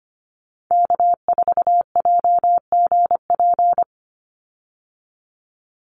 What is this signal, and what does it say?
Morse code, heard as a keyed tone, about 25 wpm, 700 Hz. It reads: K4JGP
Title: USA Call Signs 25wpm